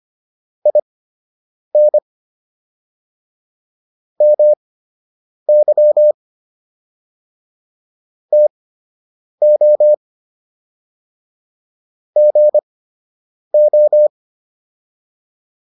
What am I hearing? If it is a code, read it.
Morse code, heard as a keyed tone, about 25 wpm, spaced as for 8 wpm, 600 Hz. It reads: IN MY TO GO